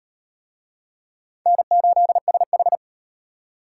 Morse code, heard as a keyed tone, about 38 wpm, 700 Hz. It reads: N8SH